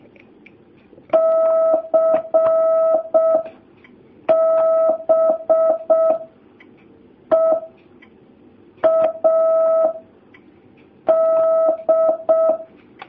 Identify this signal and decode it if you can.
Morse code, heard as a keyed tone, about 6 wpm, 650 Hz. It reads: C B E A D
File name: jh8cbh / 2019cw